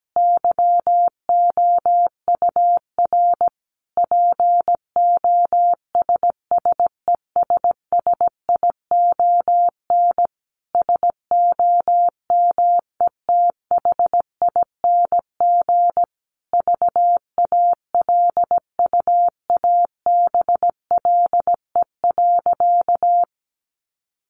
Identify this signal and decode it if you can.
Morse code, heard as a keyed tone, about 17 wpm, 700 Hz. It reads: YOUR POSSESSION SOMETHING VALUABLE.
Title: Sentences from Top 1400 Words 17wpm